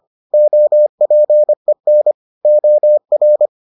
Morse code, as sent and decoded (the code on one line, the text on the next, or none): --- .--. -. / --- .-.
OPEN OR